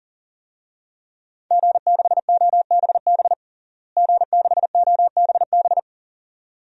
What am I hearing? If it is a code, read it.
Morse code, heard as a keyed tone, about 40 wpm, 700 Hz. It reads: G6OBB G6OBB
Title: DXCC-100 Call Signs - 5 Characters - Repeated 2x - 40wpm